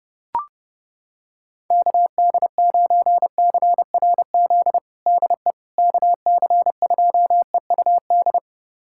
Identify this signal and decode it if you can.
Morse code, heard as a keyed tone, about 32 wpm, 700 Hz. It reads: KD9CRZ DE KC2EUD
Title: USA Callsign DE Callsign 30wpm